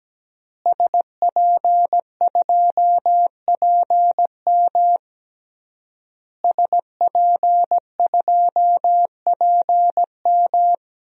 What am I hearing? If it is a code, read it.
Morse code, heard as a keyed tone, about 17 wpm, 700 Hz. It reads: SP2PM SP2PM
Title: DXCC-100 Call Signs - 5 Characters - Repeated 3x - 17wpm